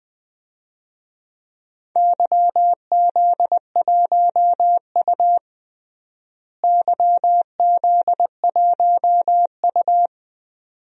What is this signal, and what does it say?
Morse code, heard as a keyed tone, about 21 wpm, 700 Hz. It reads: YZ1U YZ1U
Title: DXCC-100 Call Signs - 4 Characters - Repeated 2x - 20wpm